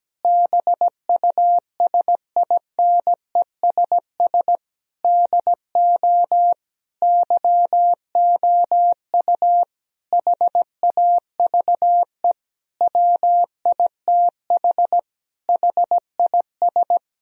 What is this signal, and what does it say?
Morse code, heard as a keyed tone, about 17 wpm, 700 Hz. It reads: BUSINESS DO YOU HAVE WITH HIS
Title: Sentences from Top 1000 Words - Review Part 3 - 17wpm